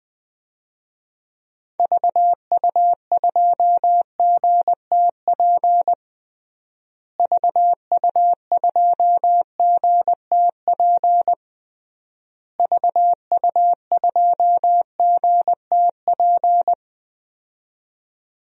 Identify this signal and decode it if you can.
Morse code, heard as a keyed tone, about 21 wpm, 700 Hz. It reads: VU2GTP VU2GTP VU2GTP
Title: DXCC-100 Call Signs - 6 Characters - Repeated 3x - 20wpm